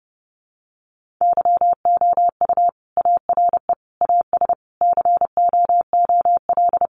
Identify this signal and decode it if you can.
Morse code, heard as a keyed tone, about 30 wpm, 700 Hz. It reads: YOU ARE AS COOL
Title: Sentences from Top 1500 Words 30wpm